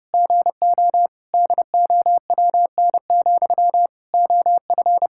Morse code, heard as a keyed tone, about 32 wpm, 700 Hz. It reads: GO DOWN, OF
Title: Sentences from Top 1000 Words - Review Part 5 - 30wpm